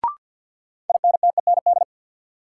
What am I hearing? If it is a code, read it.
Morse code, heard as a keyed tone, about 53 wpm, 700 Hz. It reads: INTEND
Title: Sentences from Top 1600 Words - New Word Drill - 50wpm